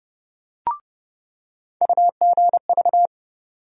Morse code, heard as a keyed tone, about 30 wpm, 700 Hz. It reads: UGV
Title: USA Call Sign Postfixes - 3 Letters 30wpm